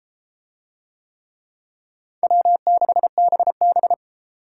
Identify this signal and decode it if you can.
Morse code, heard as a keyed tone, about 31 wpm, 700 Hz. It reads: W6BB